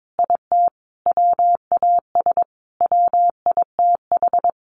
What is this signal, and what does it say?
Morse code, heard as a keyed tone, about 22 wpm, 700 Hz. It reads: IT WAS WITH
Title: Sentences from Top 700 Words 22wpm